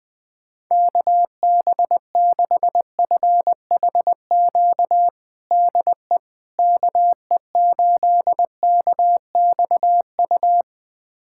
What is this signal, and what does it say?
Morse code, heard as a keyed tone, about 20 wpm, 700 Hz. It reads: KB6FHQ DE KE8KXU
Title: USA Callsign DE Callsign 20wpm